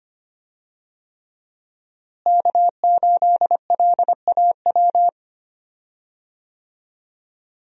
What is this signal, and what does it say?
Morse code, heard as a keyed tone, about 25 wpm, 700 Hz. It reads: K8LAW
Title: USA Call Signs - 5 Characters - Repeated 3x - 25wpm